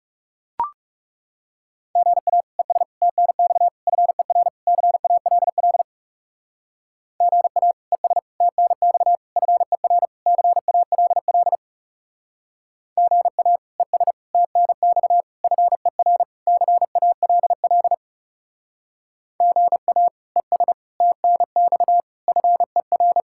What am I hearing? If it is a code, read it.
Morse code, heard as a keyed tone, about 38 wpm, 700 Hz. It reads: GA ES TNX FER CALL GA ES TNX FER CALL GA ES TNX FER CALL GA ES TNX FER
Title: Speed-Racing Common QSO Phrases Encore - 30wpm